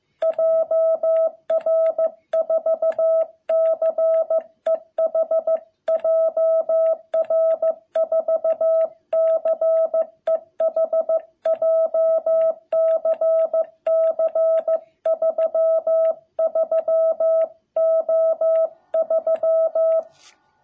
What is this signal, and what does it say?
Morse code, heard as a keyed tone, about 15 wpm, 660 Hz. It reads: JR4CEH JR4CEH JCC33O3
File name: jh8cbh / 2021cw